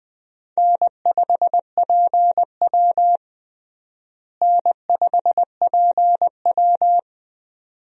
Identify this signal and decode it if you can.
Morse code, heard as a keyed tone, about 21 wpm, 700 Hz. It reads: N5PW N5PW
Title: USA Call Signs - 4 Characters - Repeated 2x - 20wpm